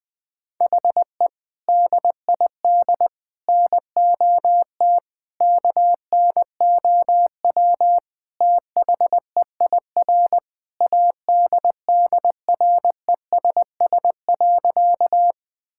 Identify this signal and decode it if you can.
Morse code, heard as a keyed tone, about 20 wpm, 700 Hz. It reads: HE DID NOT KNOW THEIR ADDRESS.
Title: Sentences from Top 1000 Words - Review Part 6 - 20wpm